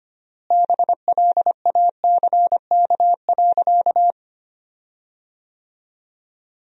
Morse code, heard as a keyed tone, about 25 wpm, 700 Hz. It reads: BLACK.
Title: 4x Word-Spacing - Sentences from Top 400 Words - 25wpm